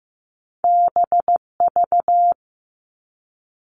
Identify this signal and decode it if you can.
Morse code, heard as a keyed tone, about 15 wpm, 700 Hz. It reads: BV
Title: All Cut Numbers - 2-Digit 15wpm